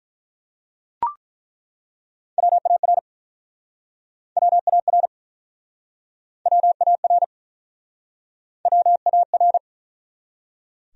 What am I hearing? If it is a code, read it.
Morse code, heard as a keyed tone, about 49 wpm, 700 Hz. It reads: WAR WAR WAR WAR